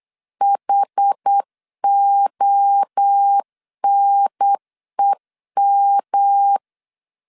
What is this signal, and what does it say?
Morse code, heard as a keyed tone, about 8 wpm, 800 Hz. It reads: HONEM